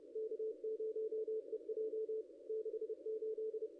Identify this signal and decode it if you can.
Morse code, heard as a keyed tone, about 30 wpm, 440 Hz. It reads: K0EJ 68
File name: contests / ss-cw-k5tr-op-n5rz -0200